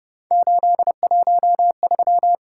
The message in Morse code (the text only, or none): 813